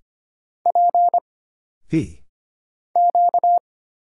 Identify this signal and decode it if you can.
Morse code, heard as a keyed tone, about 25 wpm, 700 Hz. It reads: P Q
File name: Single Character (Letter:Number:Punctuation:Prosign) - Mind-Melt - 25wpm